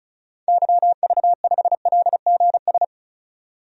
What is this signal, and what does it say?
Morse code, heard as a keyed tone, about 34 wpm, 700 Hz. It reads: YV5LGS
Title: DXCC-100 Call Signs - 6 Characters - 35wpm